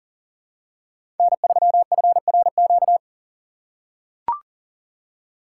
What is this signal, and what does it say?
Morse code, heard as a keyed tone, about 37 wpm, 700 Hz. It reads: N3FRQ
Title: USA Call Signs - 6x Character Spacing - 40wpm